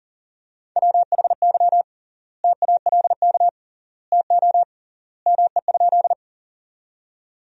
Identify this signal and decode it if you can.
Morse code, heard as a keyed tone, about 40 wpm, 700 Hz. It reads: WHY TALK TO ME?
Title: 3x Word-Spacing - Sentences from Top 400 Words - 40wpm